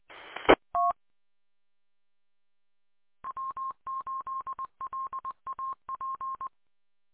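Morse code, heard as a keyed tone, about 20 wpm, 1060 Hz. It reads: ET W8LAP
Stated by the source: Morse on W8LAP 146.620 repeater.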